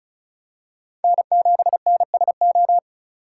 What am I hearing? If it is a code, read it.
Morse code, heard as a keyed tone, about 36 wpm, 700 Hz. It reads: N7NSO
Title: USA Call Signs - 5 Characters - 2x Character Spacing - 35wpm